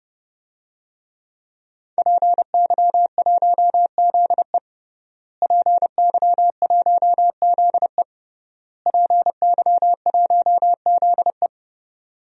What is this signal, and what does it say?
Morse code, heard as a keyed tone, about 32 wpm, 700 Hz. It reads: PY1ZE PY1ZE PY1ZE
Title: DXCC-100 Call Signs - 5 Characters - Repeated 3x - 30wpm